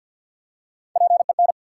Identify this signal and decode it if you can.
Morse code, heard as a keyed tone, about 48 wpm, 700 Hz. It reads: PEN
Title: Three Letter Words - 50wpm